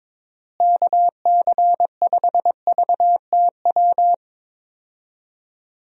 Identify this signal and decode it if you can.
Morse code, heard as a keyed tone, about 22 wpm, 700 Hz. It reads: KC5VTW
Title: USA Call Signs - 6 Characters - Repeated 2x - 22wpm